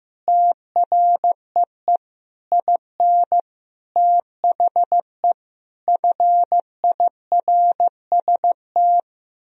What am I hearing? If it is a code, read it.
Morse code, heard as a keyed tone, about 15 wpm, 700 Hz. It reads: TREE IN THE FIRST
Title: Sentences from Top 1000 Words - Review Part 1 - 15wpm